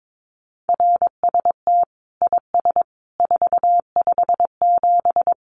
Morse code, heard as a keyed tone, about 22 wpm, 700 Hz. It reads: RST IS 457
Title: Common QSO Phrases 22wpm